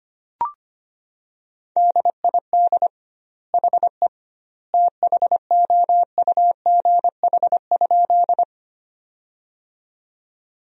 Morse code, heard as a keyed tone, about 25 wpm, 700 Hz. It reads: DID HE THOUGH?
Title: 2x Word-Spacing - Sentences from Top 400 Words - 25wpm